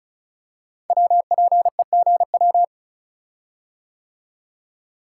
Morse code, heard as a keyed tone, about 34 wpm, 700 Hz. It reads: WPEGW
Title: Code Groups - 5 Letter:Numbers - 35wpm